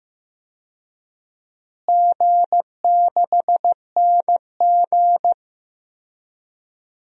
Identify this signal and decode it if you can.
Morse code, heard as a keyed tone, about 15 wpm, 700 Hz. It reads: G6NG
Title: DXCC-100 Call Signs - 4 Characters - Repeated 2x - 15wpm